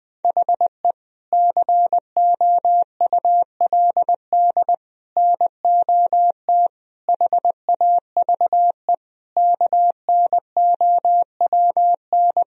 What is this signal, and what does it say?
Morse code, heard as a keyed tone, about 21 wpm, 700 Hz. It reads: HE COULD NOT HAVE KNOWN
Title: Sentences from Top 1000 Words - Review Part 3 - 20wpm